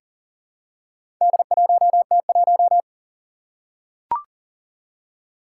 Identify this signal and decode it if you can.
Morse code, heard as a keyed tone, about 40 wpm, 700 Hz. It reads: D1T1
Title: Four Challenging Letter-Number Combinations 40wpm